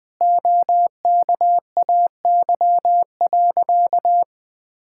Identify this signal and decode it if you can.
Morse code, heard as a keyed tone, about 20 wpm, 700 Hz. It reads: OKAY.
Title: Sentences with 5-Word Phrases - 20wpm